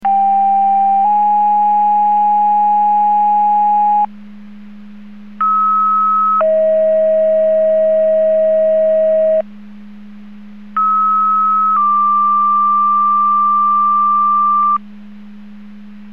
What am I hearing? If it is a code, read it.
Two-tone page, heard as a one-second tone, then a three-second tone, about 790 then 830 Hz, about 1290 then 640 Hz, about 1290 then 1180 Hz, with noise. It is heard from 0.0 to 14.8 s.
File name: Portage / 00001 04-55 Portage 154.385 Paging